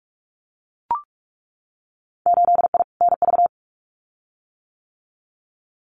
Morse code, heard as a keyed tone, about 44 wpm, 700 Hz. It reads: 8I NV
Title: Winter Field Day Exchanges Encore 45wpm